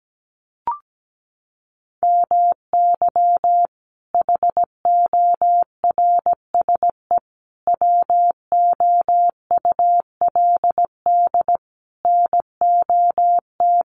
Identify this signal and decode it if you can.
Morse code, heard as a keyed tone, about 17 wpm, 700 Hz. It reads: MY HORSE WOULD NOT
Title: Sentences from Top 1400 Words 17wpm